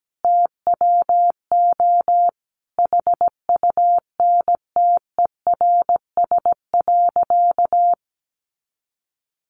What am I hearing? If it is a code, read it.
Morse code, heard as a keyed tone, about 17 wpm, 700 Hz. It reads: TWO HUNTERS.
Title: A Sound Of Thunder 17wpm